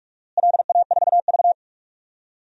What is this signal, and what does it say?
Morse code, heard as a keyed tone, about 41 wpm, 700 Hz. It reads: LA4V
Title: DXCC-100 Call Signs - 4 Characters - Repeated 3x - 45wpm